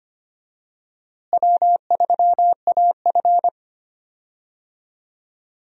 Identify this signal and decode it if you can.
Morse code, heard as a keyed tone, about 25 wpm, 700 Hz. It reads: W3AF
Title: USA Call Signs - 4 Characters - Repeated 2x - 25wpm